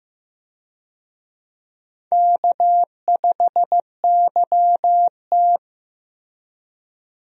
Morse code, heard as a keyed tone, about 15 wpm, 700 Hz. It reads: K5YT
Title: USA Call Signs - 4 Characters - Repeated 2x - 15wpm